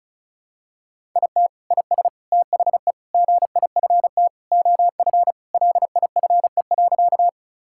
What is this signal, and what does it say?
Morse code, heard as a keyed tone, about 36 wpm, 700 Hz. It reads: IT IS THE GIFT OF LIFE.
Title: Sentences from Top 1000 Words - Review Part 6 - 35wpm